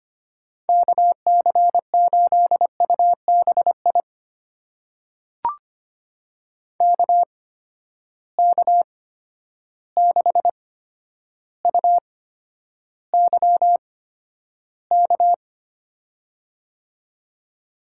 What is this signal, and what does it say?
Morse code, heard as a keyed tone, about 25 wpm, spaced as for 11 wpm, 700 Hz. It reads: KC8UBI KK6UYK
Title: USA Call Signs - 6 Characters - 8x Character Spacing - 25wpm